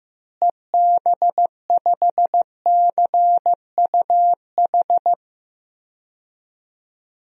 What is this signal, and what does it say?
Morse code, heard as a keyed tone, about 15 wpm, 700 Hz. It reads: EB5CUH
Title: DXCC-100 Call Signs - 6 Characters - Repeated 3x - 15wpm